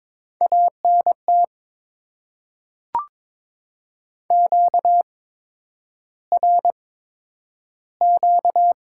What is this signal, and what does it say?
Morse code, heard as a keyed tone, about 21 wpm, spaced as for 9 wpm, 700 Hz. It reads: ANT QRQ
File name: Top 90 QSO Elements - 8x Character Spacing - 22wpm